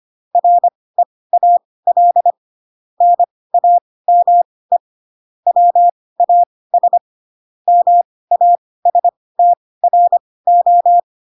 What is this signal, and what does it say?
Morse code, heard as a keyed tone, about 25 wpm, spaced as for 18 wpm, 700 Hz. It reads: REAL NAME WAS MASTRO
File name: Pinocchio / Pinocchio ch1 12 25 ik1hge